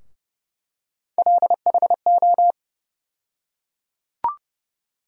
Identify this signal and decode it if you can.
Morse code, heard as a keyed tone, about 30 wpm, 700 Hz. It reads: LHO
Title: USA Call Sign Postfixes - 3 Letters 30wpm